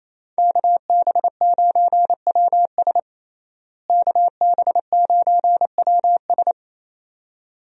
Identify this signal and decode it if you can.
Morse code, heard as a keyed tone, about 28 wpm, 700 Hz. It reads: KB9WS KB9WS
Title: USA Call Signs - 5 Characters - Repeated 2x - 28wpm